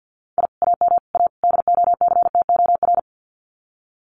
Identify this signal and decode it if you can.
Morse code, heard as a keyed tone, about 50 wpm, 700 Hz. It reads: I AM A DOCTOR
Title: Common QSO Phrases Encore 50wpm